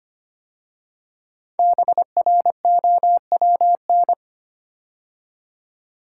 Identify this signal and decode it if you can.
Morse code, heard as a keyed tone, about 25 wpm, 700 Hz. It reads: BROWN
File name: 5 Letter Words - First 3 Letters Given - 25wpm